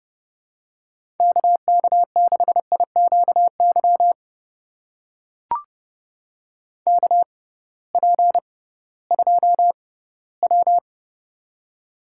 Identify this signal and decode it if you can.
Morse code, heard as a keyed tone, about 30 wpm, spaced as for 17 wpm, 700 Hz. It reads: KK6IQY KP2W
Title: USA Call Signs - 6x Character Spacing - 30wpm